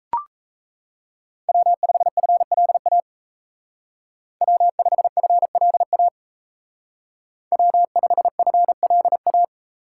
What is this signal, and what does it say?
Morse code, heard as a keyed tone, about 34 wpm, 700 Hz. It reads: W5FLA W5FLA W5FLA
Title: Speed Racing - USA 1x3 Call Signs 28wpm